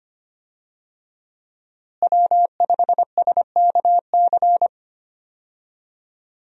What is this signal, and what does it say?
Morse code, heard as a keyed tone, about 25 wpm, 700 Hz. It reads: W5SKC
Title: USA Call Signs - 5 Characters - Repeated 2x - 25wpm